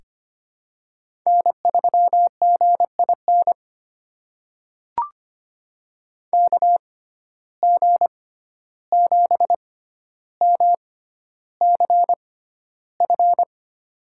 Morse code, heard as a keyed tone, about 25 wpm, spaced as for 12 wpm, 700 Hz. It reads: N3GIN KG7MCF
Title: USA Call Signs - 6x Character Spacing - 25wpm